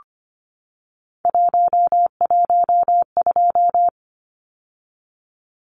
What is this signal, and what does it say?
Morse code, heard as a keyed tone, about 25 wpm, 700 Hz. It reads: 112
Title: Numbers - 3-Digit 25wpm